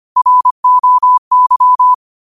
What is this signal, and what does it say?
Morse code, heard as a keyed tone, about 25 wpm, 1000 Hz. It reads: ROY